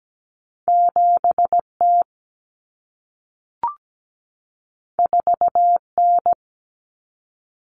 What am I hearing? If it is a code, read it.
Morse code, heard as a keyed tone, about 17 wpm, 700 Hz. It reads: 7T 4N
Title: Common Cut Numbers - 2-Digit 17wpm